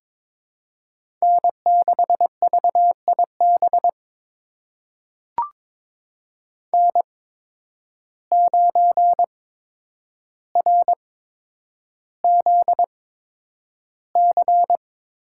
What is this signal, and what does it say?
Morse code, heard as a keyed tone, about 22 wpm, spaced as for 9 wpm, 700 Hz. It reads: N6VIB N9RZC
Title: USA Call Signs - 8x Character Spacing - 22wpm